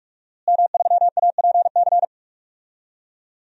Morse code, heard as a keyed tone, about 46 wpm, 700 Hz. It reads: M3APC
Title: DXCC-100 Call Signs - 5 Characters - Repeated 3x - 45wpm